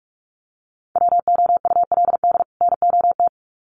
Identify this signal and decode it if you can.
Morse code, heard as a keyed tone, about 46 wpm, 700 Hz. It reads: WOULD NOT
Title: Sets of 2 Words 45wpm